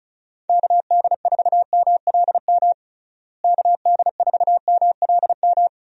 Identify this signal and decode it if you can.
Morse code, heard as a keyed tone, about 36 wpm, 700 Hz. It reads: KD4MLM KD4MLM
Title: USA Call Signs - 6 Characters - Repeated 2x - 35wpm